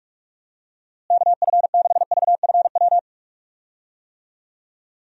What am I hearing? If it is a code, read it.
Morse code, heard as a keyed tone, about 46 wpm, 700 Hz. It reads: KF6VFW
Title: USA Call Signs - 6 Characters - Repeated 3x - 45wpm